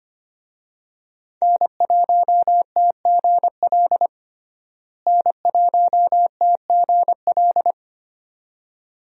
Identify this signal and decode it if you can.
Morse code, heard as a keyed tone, about 25 wpm, 700 Hz. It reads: N1TGL N1TGL
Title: USA Call Signs - 5 Characters - Repeated 2x - 25wpm